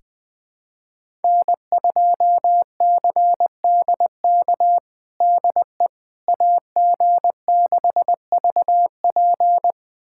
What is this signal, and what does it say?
Morse code, heard as a keyed tone, about 21 wpm, 700 Hz. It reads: N2CDK DE AG6VP
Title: USA Callsign DE Callsign - 5 Characters - 20wpm